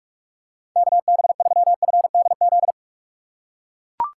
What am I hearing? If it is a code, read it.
Morse code, heard as a keyed tone, about 42 wpm, 700 Hz. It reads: KB3FDZ
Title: USA Call Signs - 6 Characters - 4x Character Spacing - 45wpm